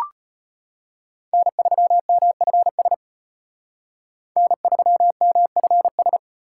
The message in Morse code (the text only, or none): N3MFS N3MFS